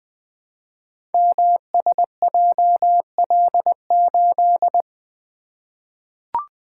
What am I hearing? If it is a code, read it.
Morse code, heard as a keyed tone, about 20 wpm, 700 Hz. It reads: MSJL8
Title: Code Groups - 5 Letter:Numbers - 20wpm